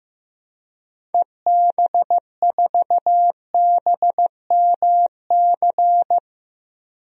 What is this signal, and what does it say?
Morse code, heard as a keyed tone, about 15 wpm, 700 Hz. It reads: EB4BMC
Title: DXCC-100 Call Signs - 6 Characters - Repeated 3x - 15wpm